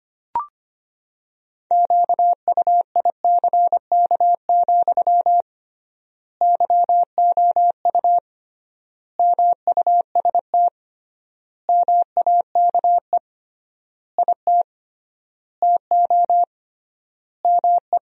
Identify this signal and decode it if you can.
Morse code, heard as a keyed tone, about 25 wpm, 700 Hz. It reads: QUICK, YOU MUST MAKE IT TO ME
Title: 3x Word-Spacing - Sentences from Top 500 Words - 25wpm